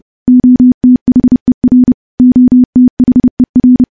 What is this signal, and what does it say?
Morse code, heard as a keyed tone, about 32 wpm, 250 Hz. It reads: OTHER OTHER
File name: morse / morse-training-30WPM